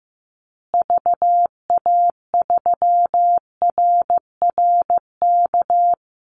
Morse code, heard as a keyed tone, about 15 wpm, 700 Hz. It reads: VA3RRK
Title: Call Signs Repeated 3x - 15wpm